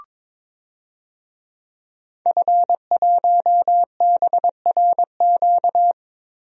Morse code, heard as a keyed tone, about 21 wpm, 700 Hz. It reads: F1BRQ
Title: DXCC-100 Call Signs - 5 Characters - Repeated 3x - 22wpm